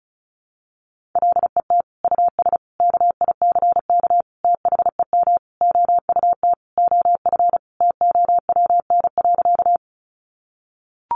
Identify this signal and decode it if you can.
Morse code, heard as a keyed tone, about 34 wpm, 700 Hz. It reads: LET US KICK THEM OUT OF TOWN.
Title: Sentences from Top 1300 Words 35wpm